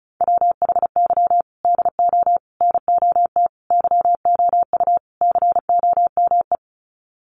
Morse code, heard as a keyed tone, about 34 wpm, 700 Hz. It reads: WHY DO NOT YOU COME
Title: Sets of 5 Words Encore2 35wpm